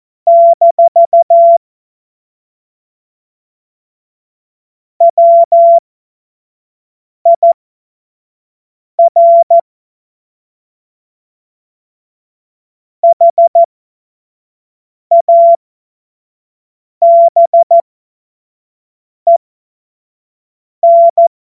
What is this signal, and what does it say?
Morse code, heard as a keyed tone, about 14 wpm, spaced as for 5 wpm, 680 Hz. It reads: - WIR HABEN
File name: Morsekurs / Witze / Witz 17. August 2021, 25 F70